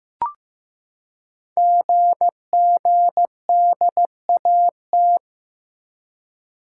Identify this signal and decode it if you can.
Morse code, heard as a keyed tone, about 15 wpm, 690 Hz. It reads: GGDAT
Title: Code Groups - 5 Letters - 15wpm